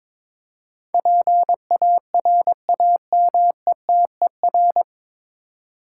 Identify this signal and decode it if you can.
Morse code, heard as a keyed tone, about 22 wpm, 700 Hz. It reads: PARAMETER
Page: Morse Code Ninja practice 22wpm